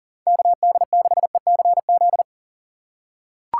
Morse code, heard as a keyed tone, about 40 wpm, 700 Hz. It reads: KD6ECZ
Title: USA Call Signs - 6 Characters - 6x Character Spacing - 40wpm